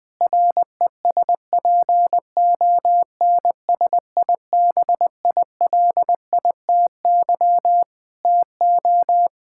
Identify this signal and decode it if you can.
Morse code, heard as a keyed tone, about 20 wpm, 700 Hz. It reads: RESPONSIBILITY TO
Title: Sentences from Top 1600 Words 20wpm